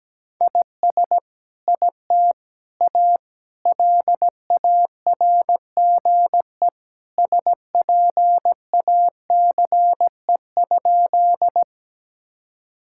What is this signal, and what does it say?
Morse code, heard as a keyed tone, about 17 wpm, 700 Hz. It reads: IS IT A LARGE SPACE?
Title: Sentences from Top 1000 Words - Review Part 5 - 17wpm